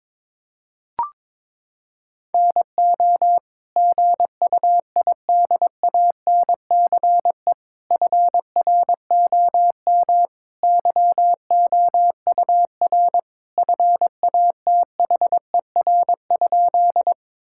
Morse code, heard as a keyed tone, about 22 wpm, 700 Hz. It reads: NO GUIDANCE FROM YOUR FATHER?
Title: Sentences from Top 1600 Words 22wpm